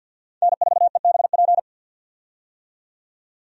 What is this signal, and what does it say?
Morse code, heard as a keyed tone, about 52 wpm, 700 Hz. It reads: N4EBP
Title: USA Call Signs - 5 Characters - 50wpm